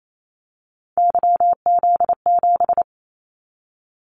Morse code, heard as a keyed tone, about 28 wpm, 700 Hz. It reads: YZ7
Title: Callsign Prefixes for Top 25 DXCC - 28wpm